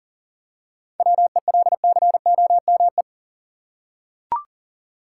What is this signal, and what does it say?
Morse code, heard as a keyed tone, about 40 wpm, 700 Hz. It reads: WELCOME
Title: Seven Letter Words - 40wpm